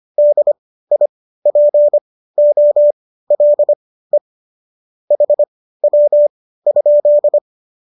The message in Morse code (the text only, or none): DIPOLE HW?